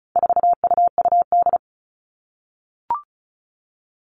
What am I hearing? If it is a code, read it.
Morse code, heard as a keyed tone, about 34 wpm, 700 Hz. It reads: 4UUD
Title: All Cut Numbers - 4-Digit 35wpm